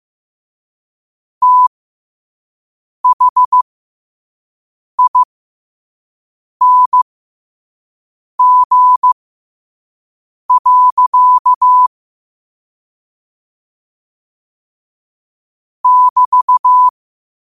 Morse code, heard as a keyed tone, about 15 wpm, spaced as for 5 wpm, 1000 Hz. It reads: THING. <BT>
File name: qotdfiles / 5wpm